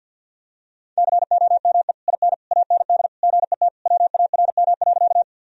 Morse code, heard as a keyed tone, about 52 wpm, 700 Hz. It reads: COME IN AND GET WARM.
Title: Sentences from Top 1000 Words - Review Part 2 - 50wpm